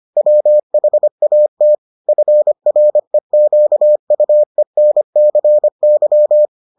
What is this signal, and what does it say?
Morse code, heard as a keyed tone, about 25 wpm, 600 Hz. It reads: WHAT FREQUENCY